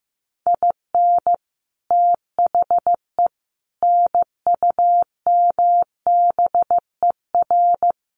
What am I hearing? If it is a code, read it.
Morse code, heard as a keyed tone, about 14 wpm, 700 Hz. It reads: IN THE NUMBER
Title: Sets of 4 Words 15wpm